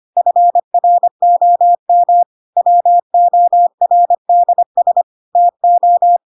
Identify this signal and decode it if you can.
Morse code, heard as a keyed tone, about 25 wpm, 700 Hz. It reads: FROM WORDS TO
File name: Pinocchio / Pinocchio ch2 25 25 ik1hge